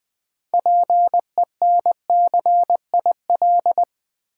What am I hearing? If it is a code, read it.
Morse code, heard as a keyed tone, about 20 wpm, 700 Hz. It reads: PENCIL
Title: 6 Letter Words - First 3 Letters Given - 20wpm